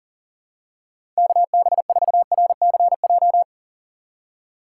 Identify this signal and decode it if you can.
Morse code, heard as a keyed tone, about 37 wpm, 700 Hz. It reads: KB4RCJ
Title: USA Call Signs - 6 Characters - 4x Character Spacing - 40wpm